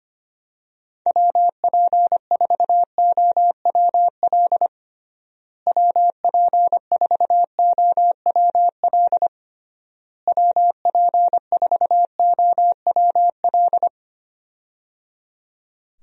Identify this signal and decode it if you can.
Morse code, heard as a keyed tone, about 25 wpm, 700 Hz. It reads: WP4OWL WP4OWL WP4OWL
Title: USA Call Signs - 6 Characters - Repeated 3x - 25wpm